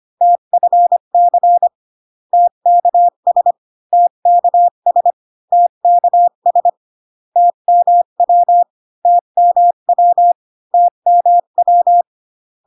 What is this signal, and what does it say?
Morse code, heard as a keyed tone, about 25 wpm, 700 Hz. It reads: TFC TKS TKS TKS TMW TMW TMW